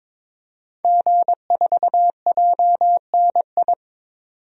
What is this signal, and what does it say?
Morse code, heard as a keyed tone, about 22 wpm, 700 Hz. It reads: G4JNI
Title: DXCC-100 Call Signs - 5 Characters - 22wpm